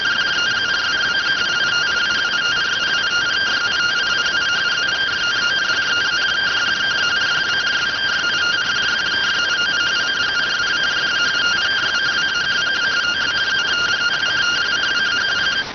Two-tone teleprinter signal, about 170 Hz shift.